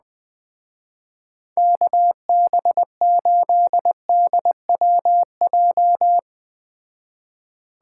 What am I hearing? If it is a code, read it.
Morse code, heard as a keyed tone, about 20 wpm, 700 Hz. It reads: KB8DWJ
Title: Speed Racing - USA 2x3 Call Signs 17wpm